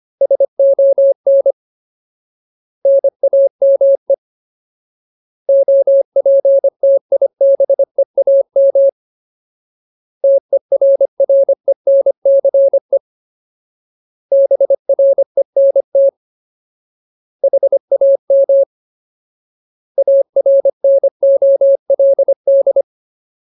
Morse code, heard as a keyed tone, about 25 wpm, 550 Hz. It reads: SON NAME OPTIBEAM TERRENCE BRENT HAM ARNOLD
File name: Ultimate Word List / 25WPM Ultimate